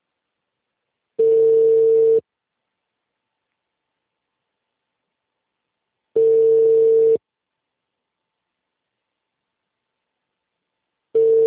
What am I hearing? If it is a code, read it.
Ringback tone (China, 450 Hz).